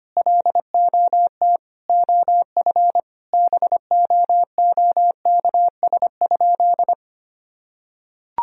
Morse code, heard as a keyed tone, about 25 wpm, 700 Hz. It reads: LOT OF BOOKS?
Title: Sentences from Top 1000 Words - Review Part 4 - 25wpm